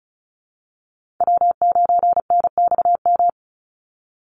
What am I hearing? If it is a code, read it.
Morse code, heard as a keyed tone, about 34 wpm, 700 Hz. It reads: W9NXM
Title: Call Signs Repeated 2x - 35wpm